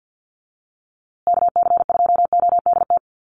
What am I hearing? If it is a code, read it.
Morse code, heard as a keyed tone, about 50 wpm, 700 Hz. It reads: KC2ODT
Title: USA Call Signs Repeated 2x 50wpm